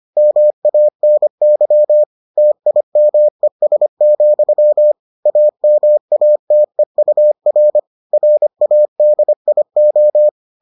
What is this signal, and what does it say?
Morse code, heard as a keyed tone, about 25 wpm, 600 Hz. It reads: MANY TIMES, AMATEUR RADIO